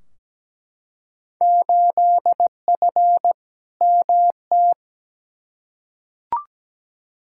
Morse code, heard as a keyed tone, about 17 wpm, 700 Hz. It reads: 8F MT
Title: Field Day Exchanges Encore - 17wpm